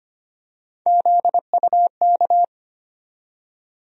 Morse code, heard as a keyed tone, about 25 wpm, 700 Hz. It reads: ZUK
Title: USA Call Sign Postfixes - 3 Letters 25wpm